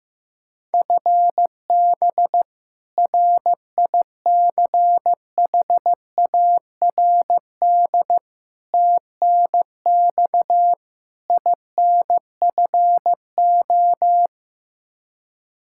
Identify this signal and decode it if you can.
Morse code, heard as a keyed tone, about 15 wpm, 700 Hz. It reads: FB RICHARD TNX INFO
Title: Practice QSO - Exchange 3 - More Info - 15wpm